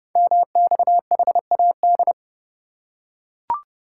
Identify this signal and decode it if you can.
Morse code, heard as a keyed tone, about 30 wpm, 700 Hz. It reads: MXHAD
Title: Code Groups - 5 Letters - 30wpm